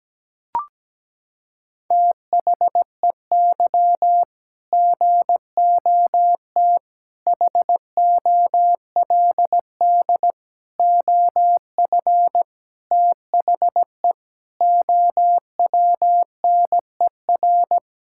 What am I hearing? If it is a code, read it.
Morse code, heard as a keyed tone, about 17 wpm, 700 Hz. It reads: THEY GOT HOLD OF THE OWNER
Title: Sentences from Top 1600 Words 17wpm